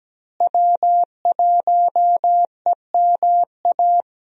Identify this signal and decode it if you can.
Morse code, heard as a keyed tone, about 17 wpm, 700 Hz. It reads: W1EMA
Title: USA Call Signs - 5 Characters - Repeated 3x - 17wpm